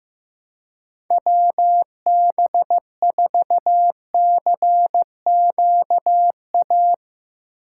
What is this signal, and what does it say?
Morse code, heard as a keyed tone, about 15 wpm, 700 Hz. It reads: WB4CQA
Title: Speed Racing - USA 2x3 Call Signs 15wpm